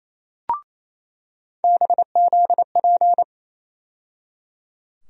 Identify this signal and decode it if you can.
Morse code, heard as a keyed tone, about 28 wpm, 700 Hz. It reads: BZP
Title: USA Call Sign Postfixes - 3 Letters 28wpm